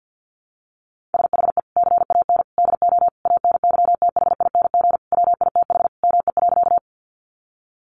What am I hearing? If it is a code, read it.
Morse code, heard as a keyed tone, about 50 wpm, 700 Hz. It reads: SHE CAN DO ANYTHING WITH ME.
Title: Sentences from Top 1000 Words 50wpm